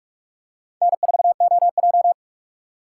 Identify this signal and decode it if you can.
Morse code, heard as a keyed tone, about 43 wpm, 700 Hz. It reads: N4OJ
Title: USA Call Signs - 4 Characters - 3x Character Spacing - 45wpm